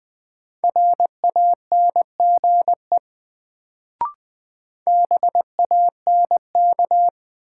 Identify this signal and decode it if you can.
Morse code, heard as a keyed tone, about 20 wpm, 700 Hz. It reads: RANGE BANK
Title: Sentences from Top 900 Words - New Word Drill - 20wpm